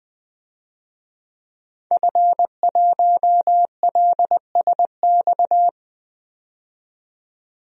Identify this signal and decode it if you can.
Morse code, heard as a keyed tone, about 20 wpm, 700 Hz. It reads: F1LSX
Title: DXCC-100 Call Signs - 5 Characters - 20wpm